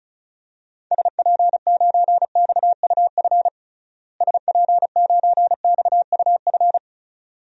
Morse code, heard as a keyed tone, about 34 wpm, 700 Hz. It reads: SP9XUF SP9XUF
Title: DXCC-100 Call Signs - 6 Characters - Repeated 2x - 35wpm